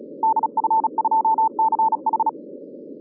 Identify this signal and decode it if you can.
Morse code, heard as a keyed tone, about 34 wpm, 890 Hz. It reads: DF2CH
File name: 2011 bcc-challenge samstag tape